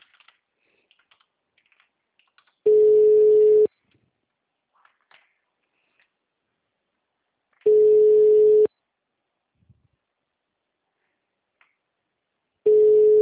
Ringback tone (Europe, 425 Hz).
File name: out- -2016- - - .5580